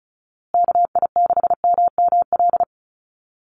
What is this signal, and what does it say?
Morse code, heard as a keyed tone, about 34 wpm, 700 Hz. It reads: KI6MML
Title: USA Call Signs Repeated 3x 35wpm